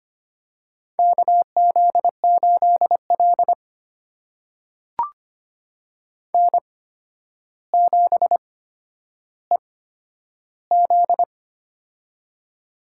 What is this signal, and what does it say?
Morse code, heard as a keyed tone, about 25 wpm, spaced as for 11 wpm, 700 Hz. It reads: KZ8L N7EZ
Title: USA Call Signs - 4 Characters - 8x Character Spacing - 25wpm